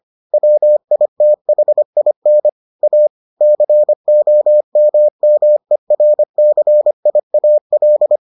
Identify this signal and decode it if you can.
Morse code, heard as a keyed tone, about 25 wpm, 600 Hz. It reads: WITHIN A COMMERCIAL